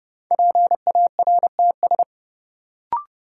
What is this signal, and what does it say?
Morse code, heard as a keyed tone, about 30 wpm, 700 Hz. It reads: PARTS
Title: Sentences from Top 1000 Words - Review Part 6 - New Word Drill - 30wpm